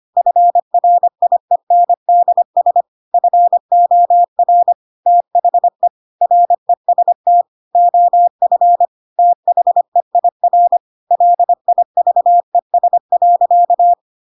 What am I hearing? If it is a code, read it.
Morse code, heard as a keyed tone, about 25 wpm, 700 Hz. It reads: FRIENDS FOR THE REST OF THEIR LIVES.
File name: Pinocchio / Pinocchio ch2 25 25 ik1hge